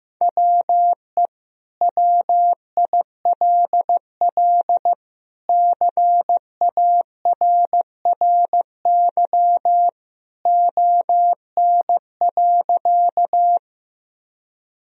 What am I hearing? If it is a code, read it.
Morse code, heard as a keyed tone, about 15 wpm, 700 Hz. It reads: WE WILL CARRY ON.
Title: Sentences from Top 1000 Words - Review Part 1 - 15wpm